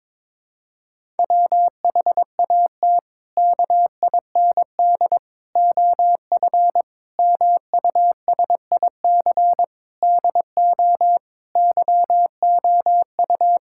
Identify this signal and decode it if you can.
Morse code, heard as a keyed tone, about 21 wpm, 700 Hz. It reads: WHAT KIND OF MUSIC DO YOU
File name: practice / 22wpm / Sentences with 3-Word Phrases - 22wpm